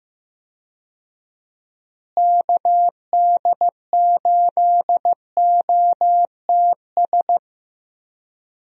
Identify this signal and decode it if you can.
Morse code, heard as a keyed tone, about 15 wpm, 700 Hz. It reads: KD8OTS
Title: USA Call Signs - 6 Characters - Repeated 3x - 15wpm